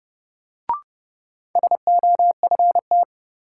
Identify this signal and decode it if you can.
Morse code, heard as a keyed tone, about 30 wpm, 700 Hz. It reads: SOFT
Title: Sentences from Top 1500 Words - New Word Drill Alt - 30wpm